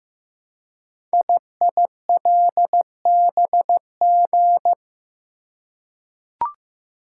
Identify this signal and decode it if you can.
Morse code, heard as a keyed tone, about 15 wpm, 700 Hz. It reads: IILBG
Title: Code Groups - 5 Letters - 15wpm